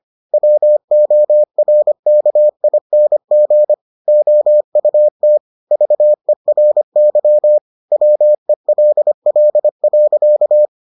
Morse code, heard as a keyed tone, about 25 wpm, 600 Hz. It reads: WORKING OUT VERY WELL.